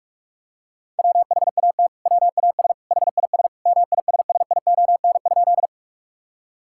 Morse code, heard as a keyed tone, about 46 wpm, 700 Hz. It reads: WHAT WAS HIS MISSION?
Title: Sentences from Top 1600 Words 45wpm